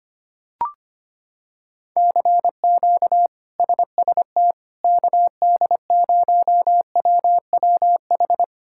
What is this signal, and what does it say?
Morse code, heard as a keyed tone, about 25 wpm, 700 Hz. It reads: CQ SST KD0WWH
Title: K1USN SST QSO Practice - 25wpm